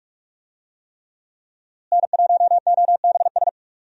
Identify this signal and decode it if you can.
Morse code, heard as a keyed tone, about 46 wpm, 700 Hz. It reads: N1OBS
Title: USA Call Signs - 5 Characters - Repeated 3x - 45wpm